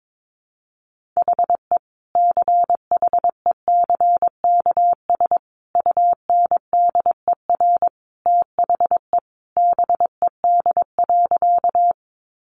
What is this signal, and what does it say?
Morse code, heard as a keyed tone, about 22 wpm, 700 Hz. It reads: HE CHECKS UNDER THE BED.
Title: Easy Phrases 22wpm